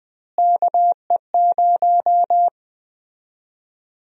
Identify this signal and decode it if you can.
Morse code, heard as a keyed tone, about 20 wpm, 700 Hz. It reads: KE0
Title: USA Call Sign Prefixes - Repeated 2x - 20wpm